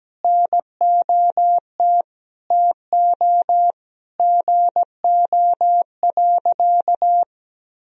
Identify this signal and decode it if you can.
Morse code, heard as a keyed tone, about 17 wpm, 700 Hz. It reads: NOT TO GO.
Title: Sentences from Top 1000 Words - Review Part 4 - 17wpm